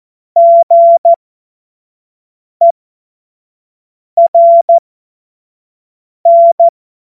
Morse code, heard as a keyed tone, about 14 wpm, spaced as for 5 wpm, 680 Hz. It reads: GERN
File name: Morsekurs / Witze / Witz 01. Februar 2022 25 F70